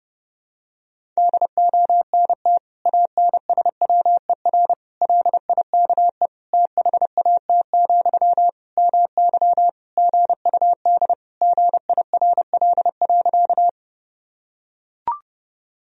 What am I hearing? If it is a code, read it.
Morse code, heard as a keyed tone, about 30 wpm, 700 Hz. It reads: DONT ANSWER LIKE THAT, MY GUD GIRL.
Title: Sentences from Top 400 Words and Abbreviations 30wpm